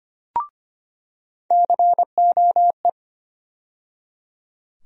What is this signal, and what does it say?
Morse code, heard as a keyed tone, about 25 wpm, 700 Hz. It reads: COE
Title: USA Call Sign Postfixes - 3 Letters 25wpm